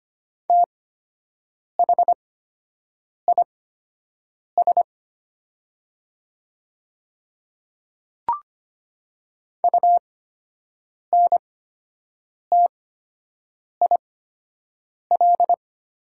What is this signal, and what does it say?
Morse code, heard as a keyed tone, about 25 wpm, spaced as for 7 wpm, 700 Hz. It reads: THIS UNTIL
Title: 8x Character Spacing - Top 500 Words - 25wpm